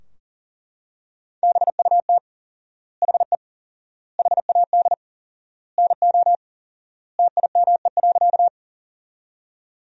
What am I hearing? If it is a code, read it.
Morse code, heard as a keyed tone, about 40 wpm, 700 Hz. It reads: BUT HE HAD NO TIME.
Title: 4x Word-Spacing - Sentences from Top 100 Words - 40wpm